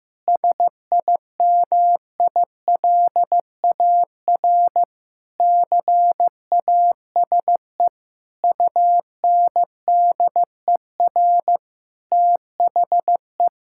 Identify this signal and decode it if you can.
Morse code, heard as a keyed tone, about 16 wpm, 700 Hz. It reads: SIMILAR CASE UNDER THE
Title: Sentences from Top 1000 Words - Review Part 5 - 15wpm